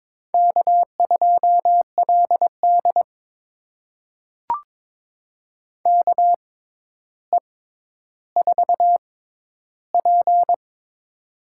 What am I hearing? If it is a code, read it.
Morse code, heard as a keyed tone, about 23 wpm, spaced as for 11 wpm, 700 Hz. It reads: K2LD KE4P
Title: USA Call Signs - 4 Characters - 6x Character Spacing - 22wpm